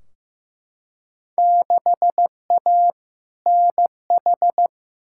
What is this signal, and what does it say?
Morse code, heard as a keyed tone, about 15 wpm, 700 Hz. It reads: 6A NH
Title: Field Day Exchanges - 15wpm